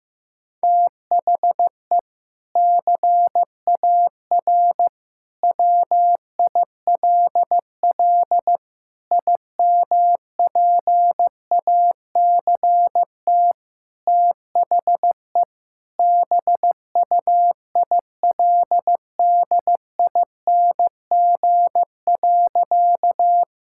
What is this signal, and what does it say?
Morse code, heard as a keyed tone, about 15 wpm, 700 Hz. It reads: THE CAR WILL IMPACT THE BUILDING.
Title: Sentences from Top 1000 Words - Review Part 6 - 15wpm